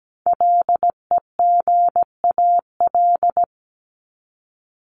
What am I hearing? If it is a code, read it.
Morse code, heard as a keyed tone, about 17 wpm, 700 Hz. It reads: LEGAL
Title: Top 1001-1250 Words 17wpm